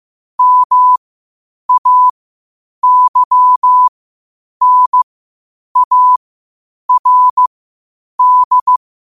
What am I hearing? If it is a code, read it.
Morse code, heard as a keyed tone, about 15 wpm, spaced as for 8 wpm, 1000 Hz. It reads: MAYNARD
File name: qotdfiles / 8wpm